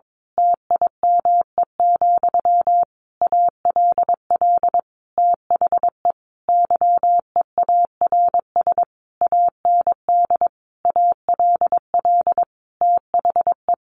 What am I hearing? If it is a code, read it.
Morse code, heard as a keyed tone, about 22 wpm, 700 Hz. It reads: TIME, ALL THE YEARS AND ALL THE
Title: A Sound Of Thunder 22wpm